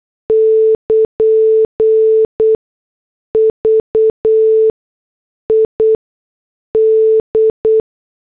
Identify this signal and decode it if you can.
Morse code, heard as a keyed tone, about 8 wpm, 440 Hz. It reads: <KN> V I D